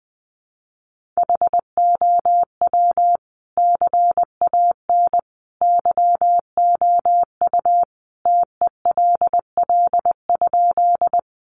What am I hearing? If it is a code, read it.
Morse code, heard as a keyed tone, about 20 wpm, 700 Hz. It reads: HOW CAN YOU TELL?
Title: Sentences from Top 200 Words 20wpm